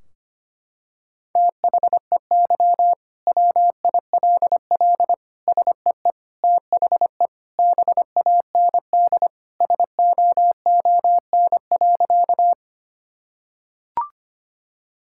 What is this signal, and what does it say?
Morse code, heard as a keyed tone, about 25 wpm, 700 Hz. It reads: THEY WILL SEE THE BAND SOON.
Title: Sentences from Top 1600 Words 25wpm